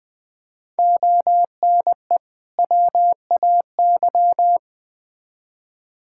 Morse code, heard as a keyed tone, about 21 wpm, 700 Hz. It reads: ONE WAY
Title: Build A Phrase - 5-Word Phrases - 20wpm